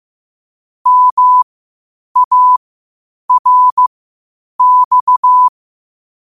Morse code, heard as a keyed tone, about 15 wpm, spaced as for 8 wpm, 1000 Hz. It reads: MARX
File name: qotdfiles / 8wpm